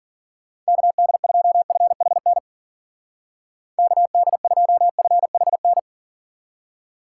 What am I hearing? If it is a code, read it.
Morse code, heard as a keyed tone, about 53 wpm, 700 Hz. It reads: KD2FHN KD2FHN